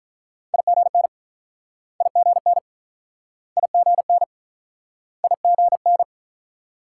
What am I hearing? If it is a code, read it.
Morse code, heard as a keyed tone, about 49 wpm, 690 Hz. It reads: IGN IGN IGN IGN